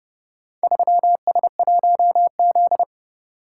Morse code, heard as a keyed tone, about 29 wpm, 700 Hz. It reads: 3S1Z